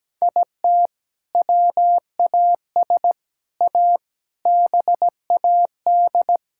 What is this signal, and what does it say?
Morse code, heard as a keyed tone, about 17 wpm, 700 Hz. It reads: IT WAS A BAD
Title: Sentences from Top 1000 Words - Review Part 6 - 17wpm